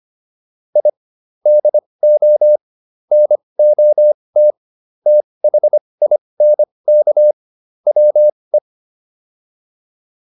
Morse code, heard as a keyed tone, about 25 wpm, spaced as for 20 wpm, 600 Hz. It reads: I DO NOT THINK WE